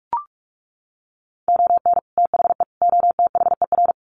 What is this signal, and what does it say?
Morse code, heard as a keyed tone, about 44 wpm, 700 Hz. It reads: ON THE OTHER
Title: Sets of 3 Words 45wpm